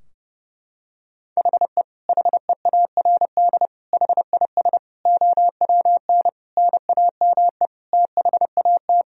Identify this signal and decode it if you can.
Morse code, heard as a keyed tone, about 30 wpm, 700 Hz. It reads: HE HEARD HIS OWN NAME THAT
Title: Sentences from Top 1000 Words - Review Part 2 - 30wpm